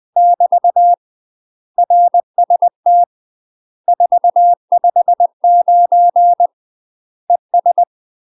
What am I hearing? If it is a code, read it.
Morse code, heard as a keyed tone, about 20 wpm, 700 Hz. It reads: <BT> RST 459 ES
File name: UK MASTER QSOs TRAINING / 20WPM / UK MASTER QSOs 0008